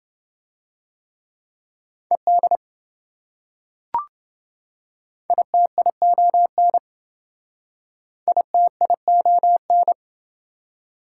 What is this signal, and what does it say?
Morse code, heard as a keyed tone, about 26 wpm, 700 Hz. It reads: ED ITION ITION